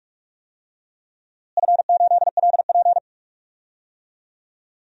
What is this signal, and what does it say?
Morse code, heard as a keyed tone, about 46 wpm, 700 Hz. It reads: F8LP
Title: DXCC-100 Call Signs - 4 Characters - Repeated 2x - 45wpm